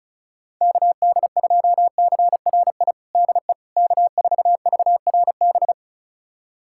Morse code, heard as a keyed tone, about 34 wpm, 700 Hz. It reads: KD2CRI DE K4VRB
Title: USA Callsign DE Callsign 35wpm